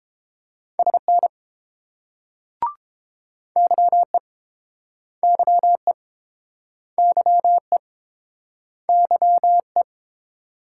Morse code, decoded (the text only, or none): SN YE YE YE YE